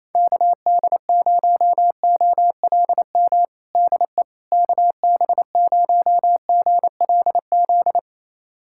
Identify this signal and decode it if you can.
Morse code, heard as a keyed tone, about 28 wpm, 700 Hz. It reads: KD0OLM DE KB0GLZ
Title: USA Callsign DE Callsign - 6 Characters - 28wpm